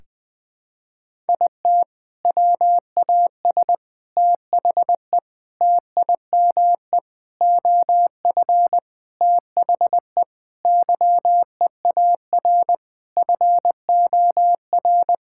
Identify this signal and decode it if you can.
Morse code, heard as a keyed tone, about 20 wpm, 700 Hz. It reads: IT WAS THE TIME OF THE YEAR FOR
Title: Sentences with 4-Word Phrases - 20wpm